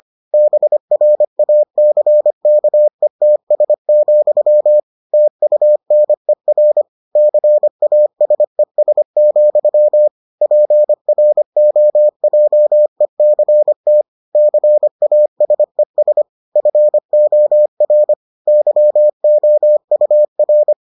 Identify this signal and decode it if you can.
Morse code, heard as a keyed tone, about 25 wpm, 600 Hz. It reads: BRACKETS, TUNER CASES, PROJECT CASES FOR YOUR